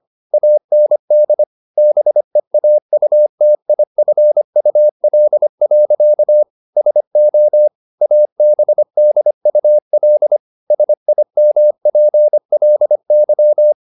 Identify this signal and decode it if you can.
Morse code, heard as a keyed tone, about 25 wpm, 600 Hz. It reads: AND BEAUTIFUL. SO ABDUL SIMPLY